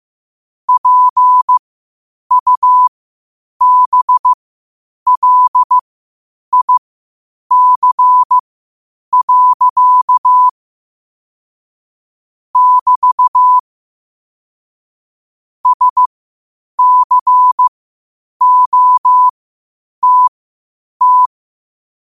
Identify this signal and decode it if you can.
Morse code, heard as a keyed tone, about 15 wpm, spaced as for 8 wpm, 1000 Hz. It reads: PUBLIC. <BT> SCOTT
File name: qotdfiles / 8wpm